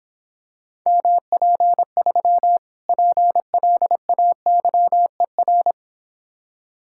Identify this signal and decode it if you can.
Morse code, heard as a keyed tone, about 26 wpm, 700 Hz. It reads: MP3 PLAYER